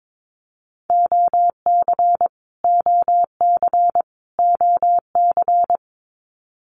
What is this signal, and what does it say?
Morse code, heard as a keyed tone, about 22 wpm, 700 Hz. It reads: OC OC OC
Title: Abbreviations Repeated 3x - 22wpm